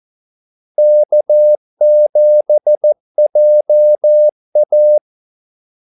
Morse code, heard as a keyed tone, about 14 wpm, 600 Hz. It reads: K7JA
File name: Callsigns-14wpm